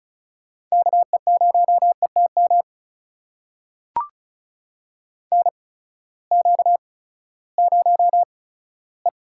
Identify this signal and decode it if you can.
Morse code, heard as a keyed tone, about 36 wpm, spaced as for 18 wpm, 700 Hz. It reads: KE0ETM NQ0E
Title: USA Call Signs - 8x Character Spacing - 35wpm